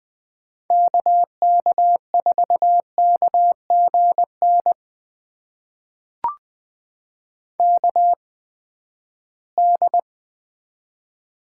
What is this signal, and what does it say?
Morse code, heard as a keyed tone, about 20 wpm, spaced as for 14 wpm, 700 Hz. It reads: KK4KGN KD
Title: USA Call Signs - 6 Characters - 8x Character Spacing - 20wpm